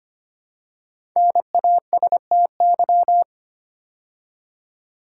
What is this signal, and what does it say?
Morse code, heard as a keyed tone, about 25 wpm, 700 Hz. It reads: NASTY
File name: 5 Letter Words - First 3 Letters Given - 25wpm